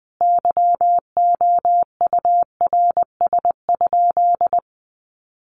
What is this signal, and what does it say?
Morse code, heard as a keyed tone, about 20 wpm, 700 Hz. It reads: YOURS?
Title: Sentences from Top 1500 Words 20wpm